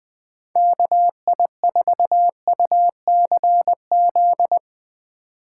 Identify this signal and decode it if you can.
Morse code, heard as a keyed tone, about 20 wpm, 700 Hz. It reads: KI4UCZ
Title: Speed Racing - USA 2x3 Call Signs 17wpm